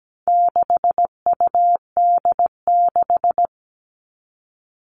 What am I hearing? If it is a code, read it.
Morse code, heard as a keyed tone, about 17 wpm, 700 Hz. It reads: 6UD6
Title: All Cut Numbers - 4-Digit 17wpm